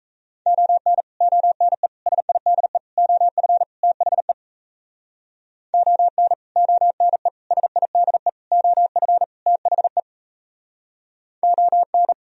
Morse code, heard as a keyed tone, about 37 wpm, 700 Hz. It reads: ON ONE SIDE OF THE ON ONE SIDE OF THE ON